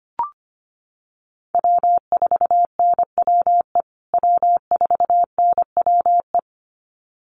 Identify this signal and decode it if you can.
Morse code, heard as a keyed tone, about 25 wpm, 700 Hz. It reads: W4NWE W4NWE
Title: USA Call Signs Repeated 2x 25wpm